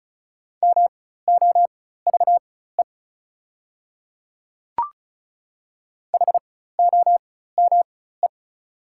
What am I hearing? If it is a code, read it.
Morse code, heard as a keyed tone, about 34 wpm, spaced as for 16 wpm, 690 Hz. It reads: MOVE HOME
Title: 4x Character Spacing - Top 200 Words - 35wpm